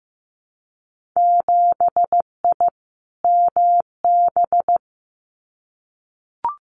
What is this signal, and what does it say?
Morse code, heard as a keyed tone, about 15 wpm, 700 Hz. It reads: 7I MB
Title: Winter Field Day Exchanges Encore 15wpm